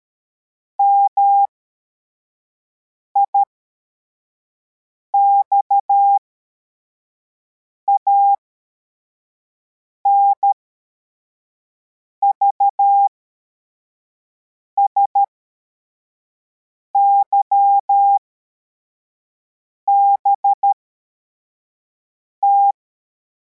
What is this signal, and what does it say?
Morse code, heard as a keyed tone, about 13 wpm, spaced as for 4 wpm, 800 Hz. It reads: MIXAN VSYBT